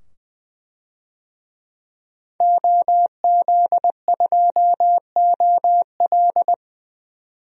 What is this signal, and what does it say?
Morse code, heard as a keyed tone, about 20 wpm, 700 Hz. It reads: OZ2OL
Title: DXCC-100 Call Signs - 5 Characters - 20wpm